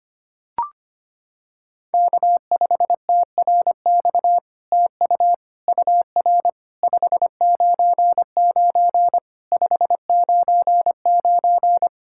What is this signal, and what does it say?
Morse code, heard as a keyed tone, about 25 wpm, 700 Hz. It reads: K5TRX TU UR 599 599
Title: SOTA QSO Practice 25wpm